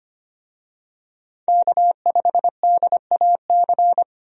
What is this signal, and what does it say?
Morse code, heard as a keyed tone, about 25 wpm, 700 Hz. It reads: K5DAC
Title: USA Call Signs - 5 Characters - 25wpm